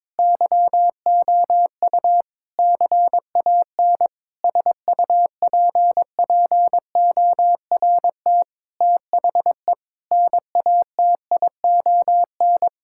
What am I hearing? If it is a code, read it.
Morse code, heard as a keyed tone, about 22 wpm, 700 Hz. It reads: YOU CAN SUPPORT THE NATION
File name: Sentences with 9-Word Phrases - 22wpm